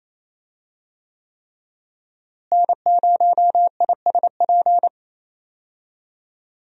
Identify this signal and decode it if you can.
Morse code, heard as a keyed tone, about 28 wpm, 700 Hz. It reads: N0ISP
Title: USA Call Signs - 5 Characters - Repeated 3x - 28wpm